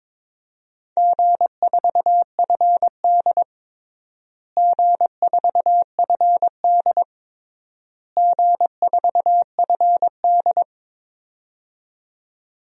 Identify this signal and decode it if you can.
Morse code, heard as a keyed tone, about 22 wpm, 700 Hz. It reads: G4FD G4FD G4FD
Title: DXCC-100 Call Signs - 4 Characters - Repeated 3x - 22wpm